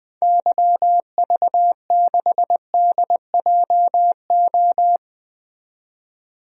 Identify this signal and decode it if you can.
Morse code, heard as a keyed tone, about 20 wpm, 700 Hz. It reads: YV6DJO
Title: DXCC-100 Call Signs - 6 Characters - 20wpm